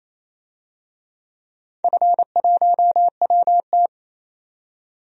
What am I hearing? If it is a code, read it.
Morse code, heard as a keyed tone, about 27 wpm, 690 Hz. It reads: F1WT
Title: DXCC-100 Call Signs - 4 Characters - 28wpm